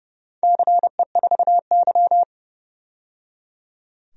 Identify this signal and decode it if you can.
Morse code, heard as a keyed tone, about 29 wpm, 700 Hz. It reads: CE4Y
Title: DXCC-100 Call Signs - 4 Characters - 30wpm